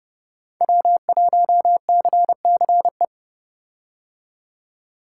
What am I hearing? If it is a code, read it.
Morse code, heard as a keyed tone, about 31 wpm, 700 Hz. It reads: W1CCE
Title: K1USN SST QSO Practice - 30wpm